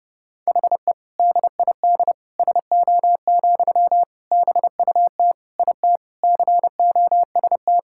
Morse code, heard as a keyed tone, about 30 wpm, 700 Hz. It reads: HE DID SO, BUT IT COST
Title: Sentences from Top 1000 Words - Review Part 5 - 30wpm